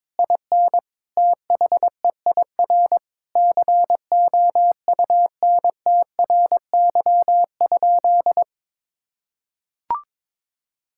Morse code, heard as a keyed tone, about 22 wpm, 700 Hz. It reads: IN THEIR COUNTRY?
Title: Sentences from Top 1000 Words - Review Part 5 - 22wpm